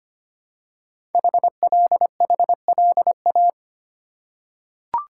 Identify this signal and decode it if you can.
Morse code, heard as a keyed tone, about 25 wpm, 700 Hz. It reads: HLHLA
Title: Code Groups - 5 Letters - 25wpm